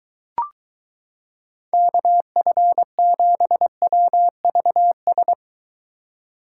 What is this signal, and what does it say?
Morse code, heard as a keyed tone, about 23 wpm, 700 Hz. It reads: KF7WVS